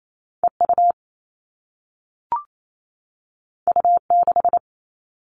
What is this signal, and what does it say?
Morse code, heard as a keyed tone, about 28 wpm, 700 Hz. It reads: EU U6
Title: All Cut Numbers - 2-Digit 28wpm